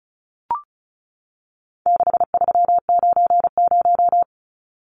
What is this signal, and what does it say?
Morse code, heard as a keyed tone, about 34 wpm, 700 Hz. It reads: 6390
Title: Numbers - 4-Digit 35wpm